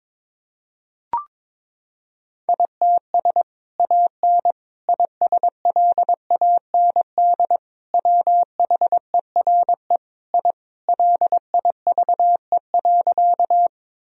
Morse code, heard as a keyed tone, about 22 wpm, 700 Hz. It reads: ITS AN ISLAND WHERE I LIVE.
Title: Sentences from Top 1000 Words - Review Part 3 - 22wpm